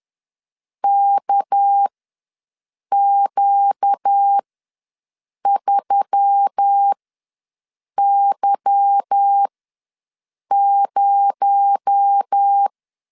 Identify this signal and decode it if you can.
Morse code, heard as a keyed tone, about 10 wpm, spaced as for 6 wpm, 800 Hz. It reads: KQ3Y0
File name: LEZIONI / 8 YJ45 / 8 Y ws 10wpm 3952